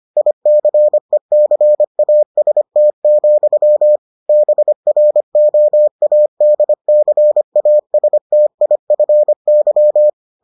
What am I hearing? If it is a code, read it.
Morse code, heard as a keyed tone, about 25 wpm, 600 Hz. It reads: ICECAST, BROADCASTIFY